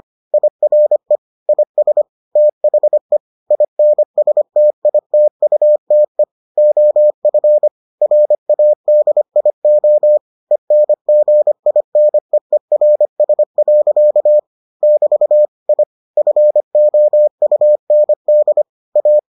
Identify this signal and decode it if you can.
Morse code, heard as a keyed tone, about 25 wpm, 600 Hz. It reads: IRE IS THE INSTITUTE OF RADIO ENGINEERS. <BT> I FOUND A